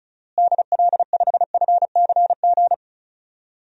Morse code, heard as a keyed tone, about 36 wpm, 700 Hz. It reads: DL5FCG
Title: DXCC-100 Call Signs - 6 Characters - Repeated 2x - 35wpm